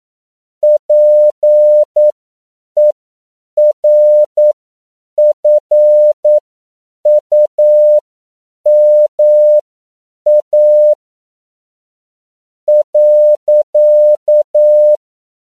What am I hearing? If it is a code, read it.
Morse code, heard as a keyed tone, about 9 wpm, spaced as for 7 wpm, 600 Hz. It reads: PERFUMA .
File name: CW / 10PPM